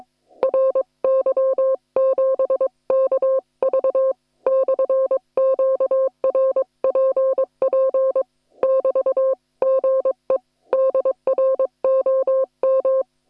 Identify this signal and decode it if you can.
Morse code, heard as a keyed tone, about 22 wpm, 530 Hz. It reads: RY7KV/QRPP <BT> GE DR OM